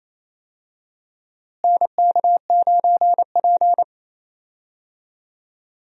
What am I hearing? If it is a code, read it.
Morse code, heard as a keyed tone, about 28 wpm, 700 Hz. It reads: NK9P
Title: USA Call Signs - 4 Characters - Repeated 2x - 28wpm